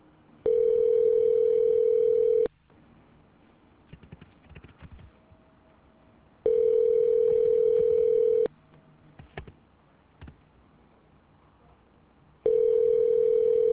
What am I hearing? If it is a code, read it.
Ringback tone (North America), 440 + 480 Hz.